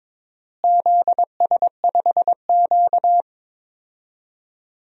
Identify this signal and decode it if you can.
Morse code, heard as a keyed tone, about 21 wpm, 700 Hz. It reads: ZS5Q
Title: DXCC-100 Call Signs - 4 Characters - Repeated 2x - 22wpm